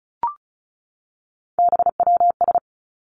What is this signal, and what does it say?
Morse code, heard as a keyed tone, about 36 wpm, 700 Hz. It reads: BWS
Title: Three Challenging Letter-Number Combinations 35wpm